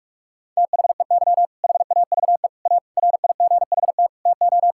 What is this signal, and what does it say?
Morse code, heard as a keyed tone, about 46 wpm, 700 Hz. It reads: THEY HAVE A RIGHT TO
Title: Sentences from Top 1000 Words - Review Part 5 - 45wpm